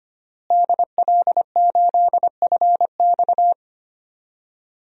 Morse code, heard as a keyed tone, about 25 wpm, 700 Hz. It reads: DL8FX
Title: DXCC-100 Call Signs - 5 Characters - Repeated 3x - 25wpm